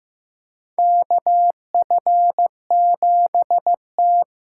Morse code, heard as a keyed tone, about 15 wpm, 700 Hz. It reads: KF7T
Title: USA Call Signs - 6 Characters - 6x Character Spacing - 15wpm